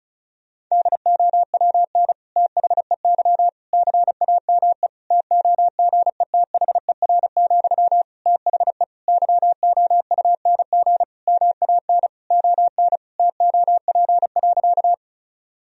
Morse code, heard as a keyed tone, about 36 wpm, 700 Hz. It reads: DOWN THEY CAME TOGETHER, THE YOUNG MAN ON TOP.
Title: Sentences from Top 1000 Words - Review Part 2 - 35wpm